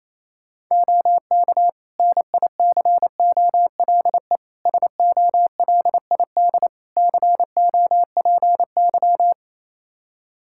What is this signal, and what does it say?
Morse code, heard as a keyed tone, about 28 wpm, 700 Hz. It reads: OK NICOLE SOLID COPY
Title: Practice QSO - Exchange 3 - More Info - 28wpm